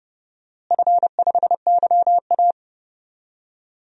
Morse code, heard as a keyed tone, about 32 wpm, 700 Hz. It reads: F5YA
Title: DXCC-100 Call Signs - 4 Characters - Repeated 3x - 30wpm